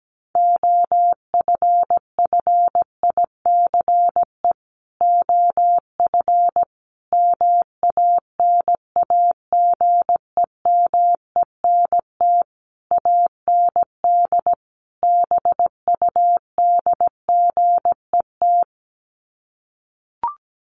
Morse code, heard as a keyed tone, about 17 wpm, 700 Hz. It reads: OFFICE OF MANAGEMENT AND BUDGET
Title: Sets of 5 Words Encore2 17wpm